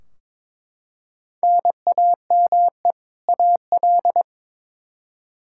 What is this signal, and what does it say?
Morse code, heard as a keyed tone, about 22 wpm, 700 Hz. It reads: NAME AL
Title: K1USN SST QSO Practice 22wpm